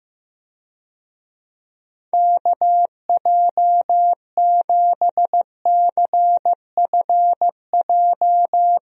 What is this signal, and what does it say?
Morse code, heard as a keyed tone, about 15 wpm, 700 Hz. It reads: KJ7CFJ
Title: USA Call Signs - 6 Characters - Repeated 2x - 15wpm